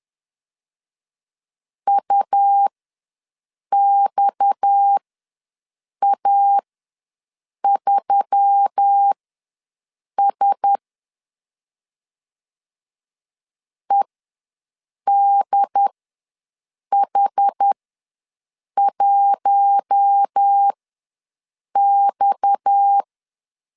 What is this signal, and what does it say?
Morse code, heard as a keyed tone, about 10 wpm, spaced as for 6 wpm, 800 Hz. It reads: UXA3S EDH1X
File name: LEZIONI / 6 PKX0 / 6 PKX ws 10wpm 3706